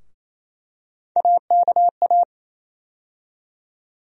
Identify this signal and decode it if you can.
Morse code, heard as a keyed tone, about 28 wpm, 700 Hz. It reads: AKA
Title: USA Call Sign Postfixes - 3 Letters 28wpm